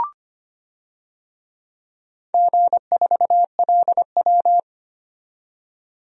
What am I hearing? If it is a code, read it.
Morse code, heard as a keyed tone, about 25 wpm, 700 Hz. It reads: G4LW
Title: DXCC-100 Call Signs - 4 Characters - 25wpm